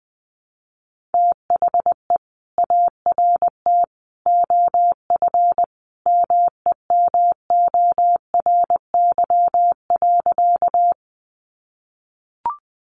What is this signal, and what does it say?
Morse code, heard as a keyed tone, about 20 wpm, 700 Hz. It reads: THE ART OF MEMORY.
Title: Sentences from Top 1300 Words 20wpm